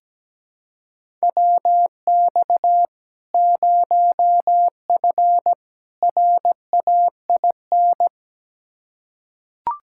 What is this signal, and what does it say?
Morse code, heard as a keyed tone, about 17 wpm, 700 Hz. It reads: WX 0F RAIN
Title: Practice QSO - Exchange 2 - Info - 17wpm